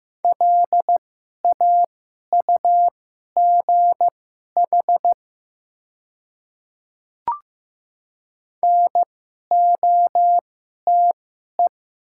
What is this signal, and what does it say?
Morse code, heard as a keyed tone, about 15 wpm, spaced as for 11 wpm, 690 Hz. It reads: LAUGH NOTE
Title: 2x Character Spacing - Top 500 Words - 15wpm